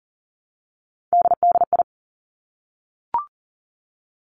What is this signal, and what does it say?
Morse code, heard as a keyed tone, about 38 wpm, 700 Hz. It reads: DDI
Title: Three Challenging Letter-Number Combinations 40wpm